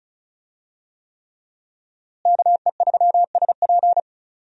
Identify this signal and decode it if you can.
Morse code, heard as a keyed tone, about 36 wpm, 700 Hz. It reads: KE3SP
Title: USA Call Signs - 5 Characters - Repeated 3x - 35wpm